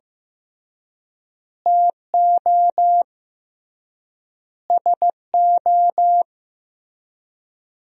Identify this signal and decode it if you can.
Morse code, heard as a keyed tone, about 15 wpm, 700 Hz. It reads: TO SO
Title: 3x Word-Spacing - Sentences from Top 400 Words - 15wpm